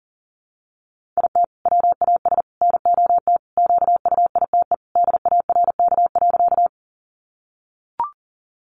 Morse code, heard as a keyed tone, about 40 wpm, 700 Hz. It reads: IT WAS NOT QUITE DARK.
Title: Sentences from Top 700 Words 40wpm